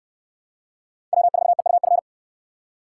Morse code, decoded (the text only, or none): W4EUF